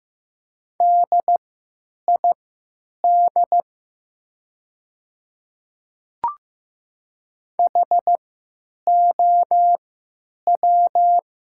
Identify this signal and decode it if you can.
Morse code, heard as a keyed tone, about 15 wpm, spaced as for 9 wpm, 700 Hz. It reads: DID HOW
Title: 3x Character Spacing - Top 100 Words - 15wpm